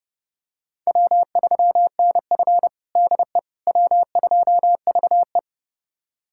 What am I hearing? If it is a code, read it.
Morse code, heard as a keyed tone, about 31 wpm, 700 Hz. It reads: W3NF DE W2VE
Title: USA Callsign DE Callsign - 4 Characters - 30wpm